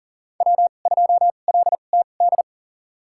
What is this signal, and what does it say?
Morse code, heard as a keyed tone, about 37 wpm, spaced as for 28 wpm, 700 Hz. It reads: W2LTD
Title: USA Call Signs - 5 Characters - 2x Character Spacing - 40wpm